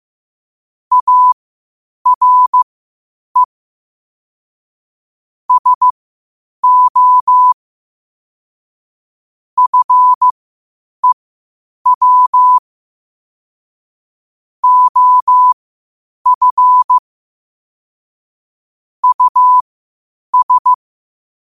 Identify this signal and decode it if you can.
Morse code, heard as a keyed tone, about 15 wpm, spaced as for 8 wpm, 1000 Hz. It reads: ARE SO FEW OF US
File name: qotdfiles / 8wpm